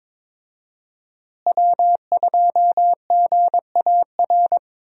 Morse code, heard as a keyed tone, about 21 wpm, 700 Hz. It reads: W2GAR
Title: USA Call Signs - 5 Characters - Repeated 2x - 22wpm